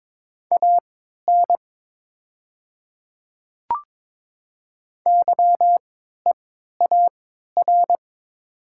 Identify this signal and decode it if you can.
Morse code, heard as a keyed tone, about 22 wpm, spaced as for 12 wpm, 700 Hz. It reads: AN YEAR
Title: 3x Character Spacing - Top 200 Words - 22wpm